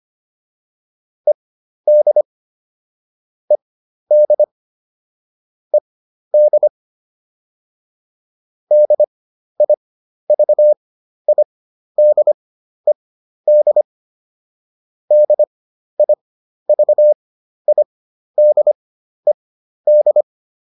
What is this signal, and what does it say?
Morse code, heard as a keyed tone, about 25 wpm, spaced as for 12 wpm, 600 Hz. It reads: ED ED ED DIVIDED DIVIDED